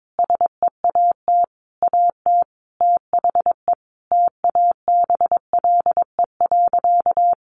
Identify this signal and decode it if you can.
Morse code, heard as a keyed tone, about 22 wpm, 700 Hz. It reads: SEAT AT THE TABLE.
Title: Sentences from Top 1500 Words 22wpm